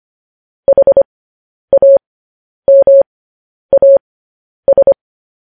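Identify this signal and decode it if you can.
Morse code, heard as a keyed tone, about 25 wpm, spaced as for 10 wpm, 550 Hz. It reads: HAMAS
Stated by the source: Morse code news headlines sent at 10 wpm for practice.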